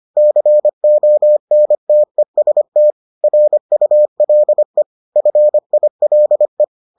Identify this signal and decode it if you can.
Morse code, heard as a keyed tone, about 25 wpm, 600 Hz. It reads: CONTEST RULE FILE